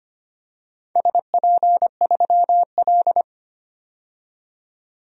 Morse code, heard as a keyed tone, about 25 wpm, 700 Hz. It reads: SP3L
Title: DXCC-100 Call Signs - 4 Characters - 25wpm